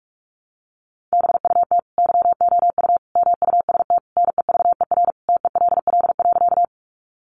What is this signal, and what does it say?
Morse code, heard as a keyed tone, about 44 wpm, 700 Hz. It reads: BUT YOU MUST NEVER TELL.
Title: Sentences from Top 300 Words 45wpm